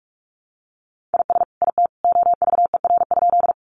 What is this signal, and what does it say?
Morse code, heard as a keyed tone, about 44 wpm, 700 Hz. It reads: IS IT OVER?
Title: Sentences from Top 100 Words 45wpm